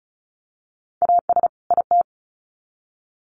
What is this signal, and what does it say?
Morse code, heard as a keyed tone, about 34 wpm, 700 Hz. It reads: AS IT
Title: Sets of 2 Words 35wpm